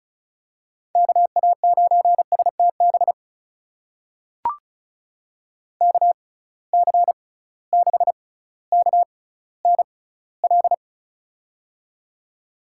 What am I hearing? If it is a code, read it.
Morse code, heard as a keyed tone, about 36 wpm, spaced as for 18 wpm, 700 Hz. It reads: KA9STB KC6KNL
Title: USA Call Signs - 6 Characters - 6x Character Spacing - 35wpm